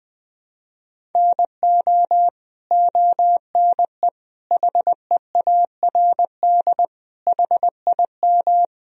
Morse code, heard as a keyed tone, about 20 wpm, 700 Hz. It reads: NO ONE HEARD HIM
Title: Sentences from Top 1000 Words - Review Part 1 - 20wpm